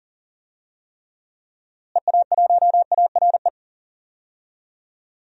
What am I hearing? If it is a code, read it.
Morse code, heard as a keyed tone, about 40 wpm, 700 Hz. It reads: EA1ARE
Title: DXCC-100 Call Signs - 6 Characters - Repeated 2x - 40wpm